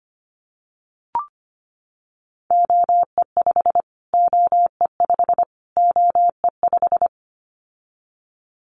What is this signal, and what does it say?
Morse code, heard as a keyed tone, about 25 wpm, 700 Hz. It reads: OE5 OE5 OE5
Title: Callsign Prefixes for Top 25 DXCC - Repeated 3x - 25wpm